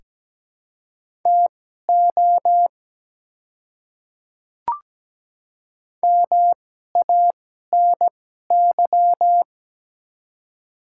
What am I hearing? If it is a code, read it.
Morse code, heard as a keyed tone, about 17 wpm, spaced as for 12 wpm, 690 Hz. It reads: TO MANY
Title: 2x Character Spacing - Top 100 Words - 17wpm